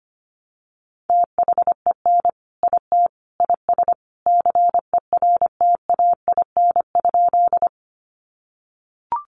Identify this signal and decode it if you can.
Morse code, heard as a keyed tone, about 25 wpm, 700 Hz. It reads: THEN IT IS CERTAIN?
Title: Sentences from Top 500 Words 25wpm